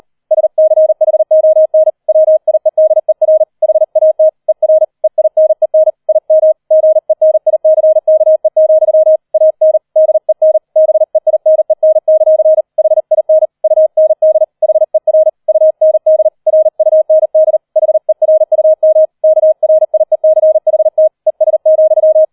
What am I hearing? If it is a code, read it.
Morse code, heard as a keyed tone, about 38 wpm, 630 Hz. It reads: SCHON WIEDER HAT ER EINEN IM GENICKE, AN DEN BEINEN; HIN UND HER UND RUND HERUM KRIECHT ES,
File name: 40wpm